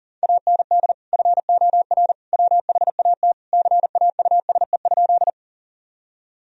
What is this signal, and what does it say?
Morse code, heard as a keyed tone, about 40 wpm, 700 Hz. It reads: AND FOR WHAT CAUSE?
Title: Sentences from Top 1000 Words - Review Part 1 - 40wpm